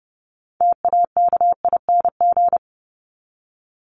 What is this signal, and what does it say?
Morse code, heard as a keyed tone, about 29 wpm, 700 Hz. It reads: TAKING
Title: ING Suffix - Top 100 Words with Base Words 30wpm